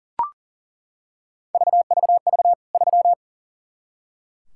Morse code, heard as a keyed tone, about 40 wpm, 700 Hz. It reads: VVV 3
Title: K1USN SST QSO Practice 40wpm